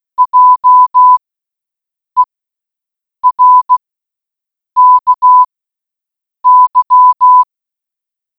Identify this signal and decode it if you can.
Morse code, heard as a keyed tone, about 16 wpm, spaced as for 7 wpm, 1000 Hz. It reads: JERKY